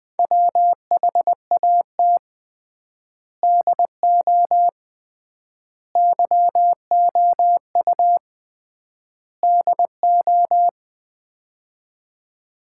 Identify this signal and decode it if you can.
Morse code, heard as a keyed tone, about 20 wpm, 700 Hz. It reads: WHAT DO YOU DO
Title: 3x Word-Spacing - Sets of 4 Words - 20wpm